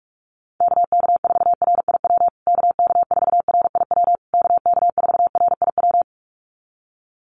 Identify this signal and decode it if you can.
Morse code, heard as a keyed tone, about 44 wpm, 700 Hz. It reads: KK4RIW KK4RIW KK4RIW
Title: USA Call Signs Repeated 3x 45wpm